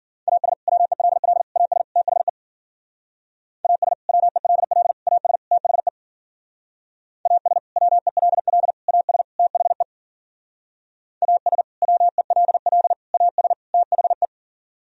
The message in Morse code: AS WELL AS THE AS WELL AS THE AS WELL AS THE AS WELL AS THE

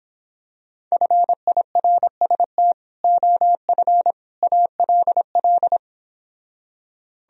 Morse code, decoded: FIRST OF ALL

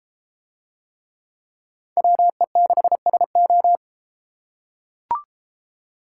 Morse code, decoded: WE6SO

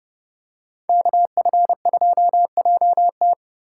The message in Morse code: KF2JT

